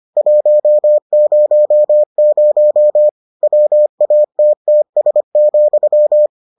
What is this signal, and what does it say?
Morse code, heard as a keyed tone, about 25 wpm, 600 Hz. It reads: 100 WATTS,